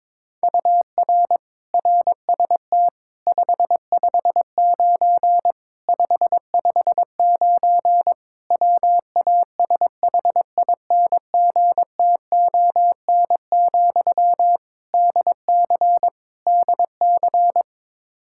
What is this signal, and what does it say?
Morse code, heard as a keyed tone, about 22 wpm, 700 Hz. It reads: UR RST 559 559 WASHINGTON, DC DC
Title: Practice QSO - Exchange 1 - Beginning - 22wpm